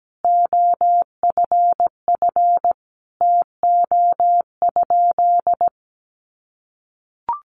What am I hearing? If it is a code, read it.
Morse code, heard as a keyed tone, about 17 wpm, 700 Hz. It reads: OFF TO?
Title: Sentences from Top 300 Words 17wpm